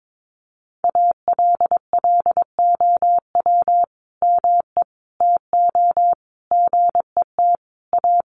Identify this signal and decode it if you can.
Morse code, heard as a keyed tone, about 22 wpm, 700 Hz. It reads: ALLOW ME TO GET A
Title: Sentences from Top 1300 Words 22wpm